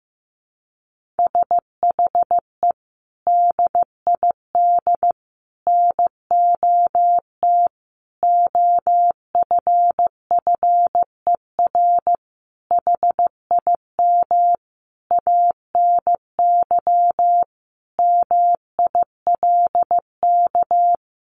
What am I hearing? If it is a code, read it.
Morse code, heard as a keyed tone, about 15 wpm, 700 Hz. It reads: SHE DID NOT OFFER HIM ANY MILK
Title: Sentences from Top 1500 Words 15wpm